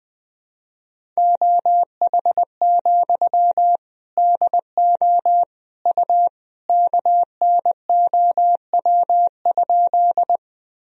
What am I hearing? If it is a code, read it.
Morse code, heard as a keyed tone, about 21 wpm, 700 Hz. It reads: OH, DO U KNOW?
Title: Sentences from Top 500 Words and Abbreviations 20wpm